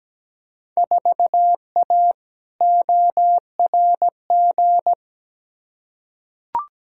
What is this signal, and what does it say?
Morse code, heard as a keyed tone, about 17 wpm, 700 Hz. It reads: 4A ORG
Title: Field Day Exchanges - 17wpm